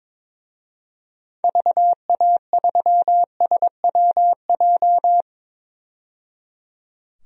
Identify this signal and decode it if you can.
Morse code, heard as a keyed tone, about 22 wpm, 700 Hz. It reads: VA3SWJ
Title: DXCC-100 Call Signs - 6 Characters - 22wpm